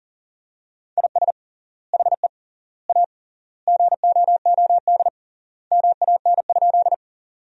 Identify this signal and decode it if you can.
Morse code, heard as a keyed tone, about 40 wpm, 700 Hz. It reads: IS HE A GOOD MAN?
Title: 3x Word-Spacing - Sentences from Top 200 Words - 40wpm